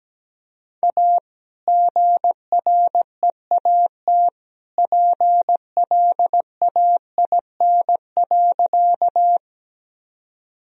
Morse code, heard as a keyed tone, about 17 wpm, 700 Hz. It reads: A GREAT PLAIN.
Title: Sentences from Top 1000 Words - Review Part 2 - 17wpm